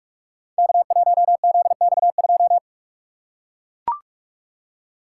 Morse code, heard as a keyed tone, about 44 wpm, 700 Hz. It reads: K1ZX2
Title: Code Groups - 5 Letter:Numbers - 45wpm